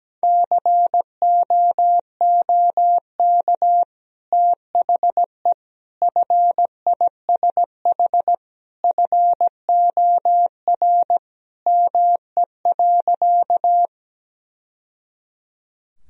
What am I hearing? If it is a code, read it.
Morse code, heard as a keyed tone, about 17 wpm, 700 Hz. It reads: COOK THE FISH FOR ME.
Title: Sentences from Top 1000 Words - Review Part 4 - 17wpm